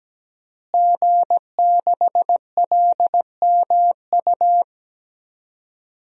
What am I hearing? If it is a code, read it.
Morse code, heard as a keyed tone, about 17 wpm, 700 Hz. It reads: G6LMU
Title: DXCC-100 Call Signs - 5 Characters - Repeated 3x - 17wpm